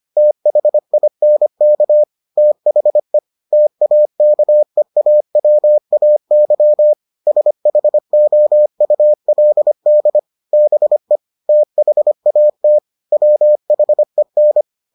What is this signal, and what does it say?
Morse code, heard as a keyed tone, about 25 wpm, 600 Hz. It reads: THINK THE TAKEAWAY SHOULD BE THAT WHEN